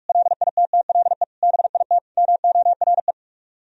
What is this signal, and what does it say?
Morse code, heard as a keyed tone, about 45 wpm, 700 Hz. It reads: LITTLE BIT MORE